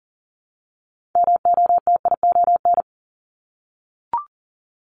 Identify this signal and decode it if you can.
Morse code, heard as a keyed tone, about 40 wpm, 700 Hz. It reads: MOTION
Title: Multi-letter Phonemes Repeated 3x 40wpm